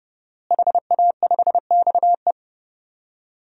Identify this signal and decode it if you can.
Morse code, heard as a keyed tone, about 30 wpm, 700 Hz. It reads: HA5XE
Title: DXCC-100 Call Signs - 5 Characters - Repeated 2x - 30wpm